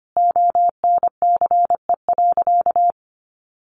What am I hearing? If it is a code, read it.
Morse code, heard as a keyed tone, about 25 wpm, 700 Hz. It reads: ONCE.
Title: Sentences from Top 400 Words 25wpm